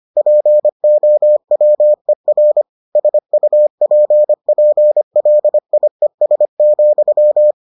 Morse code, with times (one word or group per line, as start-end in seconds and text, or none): 0.2-2.6 POWER
3.0-7.5 SUPPLIES,